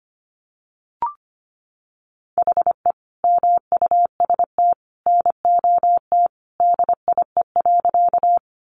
Morse code, heard as a keyed tone, about 25 wpm, 700 Hz. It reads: HE MUST NOT DIE.
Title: Sentences from Top 1300 Words 25wpm